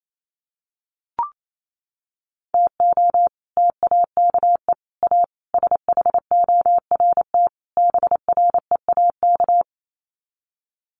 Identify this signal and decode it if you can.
Morse code, heard as a keyed tone, about 28 wpm, 700 Hz. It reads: TO TAKE A SHORT BREAK
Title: Sets of 5 Words Encore2 28wpm